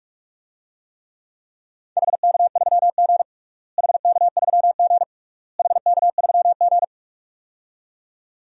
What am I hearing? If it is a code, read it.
Morse code, heard as a keyed tone, about 46 wpm, 700 Hz. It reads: HK3G HK3G HK3G
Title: DXCC-100 Call Signs - 4 Characters - Repeated 3x - 45wpm